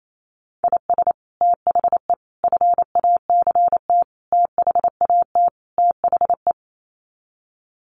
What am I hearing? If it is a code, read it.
Morse code, heard as a keyed tone, about 28 wpm, 700 Hz. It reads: IS THE FACT THAT THE
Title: Sets of 5 Words Encore2 28wpm